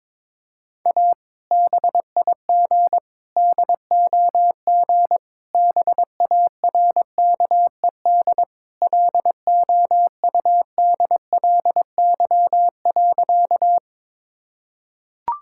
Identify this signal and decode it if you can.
Morse code, heard as a keyed tone, about 22 wpm, 700 Hz. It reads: A BIG DOG BARKED LOUDLY.
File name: practice / 22wpm / Sentences with 2-Word Phrases - 22wpm